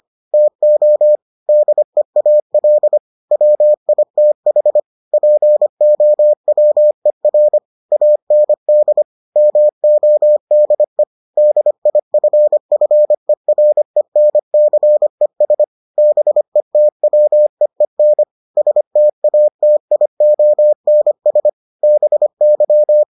Morse code, heard as a keyed tone, about 25 wpm, 600 Hz. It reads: TO DEAL WITH POWER AND MODE DIFFERENCES BETWEEN STATIONS BY